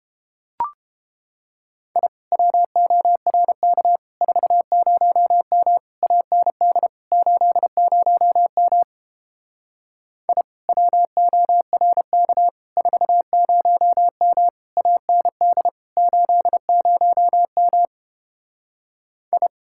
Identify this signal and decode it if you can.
Morse code, heard as a keyed tone, about 29 wpm, 700 Hz. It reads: I WORK 40M AND 80M I WORK 40M AND 80M I